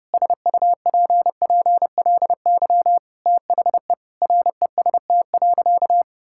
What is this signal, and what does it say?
Morse code, heard as a keyed tone, about 30 wpm, 700 Hz. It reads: SUPPLY THE REST.
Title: Sentences from Top 1000 Words - Review Part 5 - 30wpm